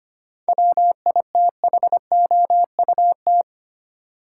Morse code, heard as a keyed tone, about 25 wpm, 700 Hz. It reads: WITHOUT
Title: Sentences from Top 600 Words - New Word Drill - 25wpm